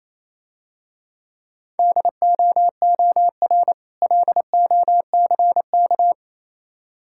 Morse code, heard as a keyed tone, about 28 wpm, 700 Hz. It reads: DOOR LOCK